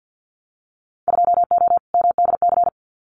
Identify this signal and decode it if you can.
Morse code, heard as a keyed tone, about 50 wpm, 700 Hz. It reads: 2O MDC
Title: Winter Field Day Exchanges Encore 50wpm